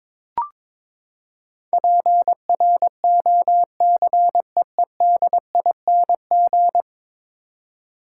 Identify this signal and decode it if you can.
Morse code, heard as a keyed tone, about 22 wpm, 700 Hz. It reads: PROCEEDING
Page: morse-code-ninja-practice-22wpm